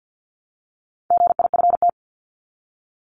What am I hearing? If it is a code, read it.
Morse code, heard as a keyed tone, about 50 wpm, 700 Hz. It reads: GIFT
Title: Top 751-1000 Words 50wpm